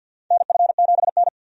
Morse code, heard as a keyed tone, about 47 wpm, 700 Hz. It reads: NF7N
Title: USA Call Signs - 4 Characters - 50wpm